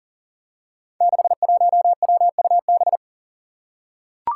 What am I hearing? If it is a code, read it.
Morse code, heard as a keyed tone, about 40 wpm, 700 Hz. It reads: 61WUB
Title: Five Challenging Letter-Number Combinations 40wpm